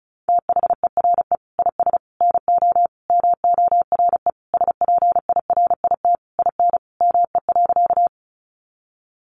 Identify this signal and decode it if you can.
Morse code, heard as a keyed tone, about 34 wpm, 700 Hz. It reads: THERE IS NO MORE SPIRIT IN ME.
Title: Sentences from Top 1400 Words 35wpm